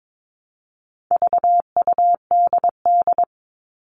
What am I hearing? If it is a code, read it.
Morse code, heard as a keyed tone, about 22 wpm, 700 Hz. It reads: VUDD
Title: All Cut Numbers - 4-Digit 22wpm